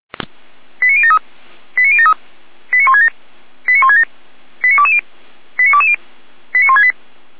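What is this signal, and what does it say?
Five-tone selective call (ZVEI), heard as quick runs of five tones, tones of about 70 ms. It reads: 89073 89073 89167 89167 89220 89220 89168
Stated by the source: ZVEI five-tone call 89022 received on SDR.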